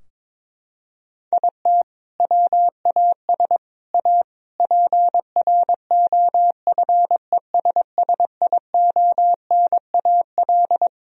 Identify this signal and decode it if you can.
Morse code, heard as a keyed tone, about 22 wpm, 700 Hz. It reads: IT WAS A PROFESSIONAL
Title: Sentences from Top 1000 Words - Review Part 4 - 22wpm